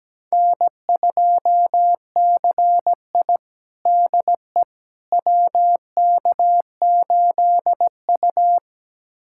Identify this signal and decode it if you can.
Morse code, heard as a keyed tone, about 17 wpm, 700 Hz. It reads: N2CI DE WK8U
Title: USA Callsign DE Callsign - 4 Characters - 17wpm